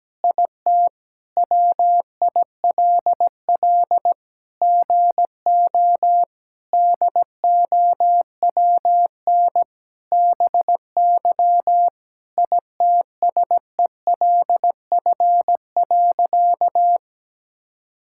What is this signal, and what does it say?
Morse code, heard as a keyed tone, about 17 wpm, 700 Hz. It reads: IT WILL GO DOWN BY ITSELF.
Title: Sentences from Top 1000 Words - Review Part 5 - 17wpm